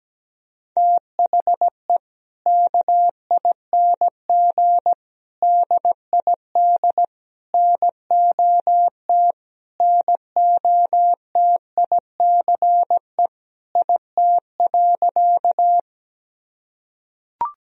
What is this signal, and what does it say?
Morse code, heard as a keyed tone, about 17 wpm, 700 Hz. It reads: THE KING DID NOT NOTICE IT.
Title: Sentences from Top 1000 Words - Review Part 2 - 17wpm